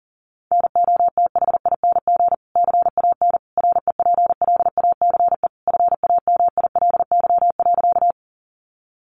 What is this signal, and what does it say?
Morse code, heard as a keyed tone, about 40 wpm, 700 Hz. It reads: NOTHING CAN REPLACE FAMILY.
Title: Sentences from Top 1400 Words 40wpm